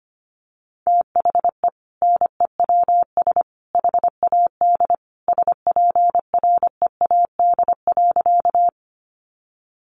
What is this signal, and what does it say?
Morse code, heard as a keyed tone, about 25 wpm, 700 Hz. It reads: THE NEWS HAD SPREAD.
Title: Sentences from Top 1200 Words 25wpm